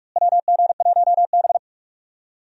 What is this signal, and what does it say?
Morse code, heard as a keyed tone, about 42 wpm, 700 Hz. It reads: WG1B
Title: USA Call Signs - 4 Characters - Repeated 2x - 45wpm